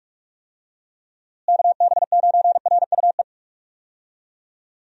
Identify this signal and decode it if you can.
Morse code, heard as a keyed tone, about 46 wpm, 700 Hz. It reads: KB9RUE
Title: USA Call Signs - 6 Characters - Repeated 3x - 45wpm